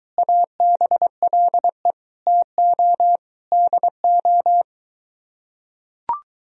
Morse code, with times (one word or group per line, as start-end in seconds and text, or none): 0.2-1.9 ABLE
2.3-3.2 TO
3.5-4.6 DO